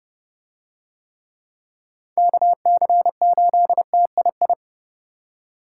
Morse code, heard as a keyed tone, about 30 wpm, 700 Hz. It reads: KC8TII